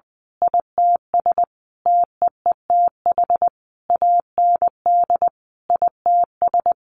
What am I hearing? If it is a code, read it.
Morse code, heard as a keyed tone, about 20 wpm, 700 Hz. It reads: ITS TEETH AND ITS
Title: A Sound Of Thunder 20wpm